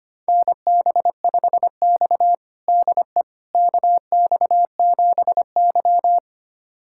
Morse code, heard as a keyed tone, about 25 wpm, 700 Hz. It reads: NB5X DE KX7Y
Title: USA Callsign DE Callsign - 4 Characters - 25wpm